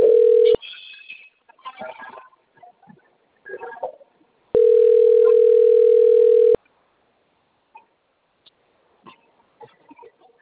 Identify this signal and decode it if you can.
Ringback tone (North America), 440 + 480 Hz.
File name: out- - - - - .5